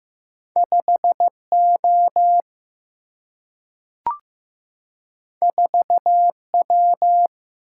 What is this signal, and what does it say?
Morse code, heard as a keyed tone, about 15 wpm, 700 Hz. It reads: 5O 4W
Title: Single-Letter Single-Number 15wpm